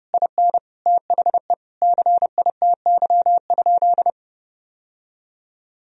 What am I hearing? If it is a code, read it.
Morse code, heard as a keyed tone, about 30 wpm, 700 Hz. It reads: IN THE CITY?
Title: Sentences from Top 1000 Words - Review Part 1 - 30wpm